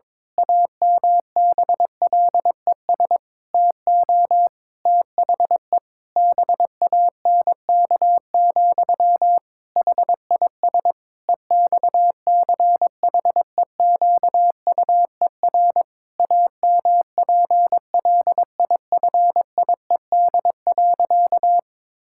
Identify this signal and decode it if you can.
Morse code, heard as a keyed tone, about 22 wpm, 700 Hz. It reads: AMBLES TO THE BANK, HIS EXCHEQUER AMPLIFIED.
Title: Pangram Challenge v2 - 22wpm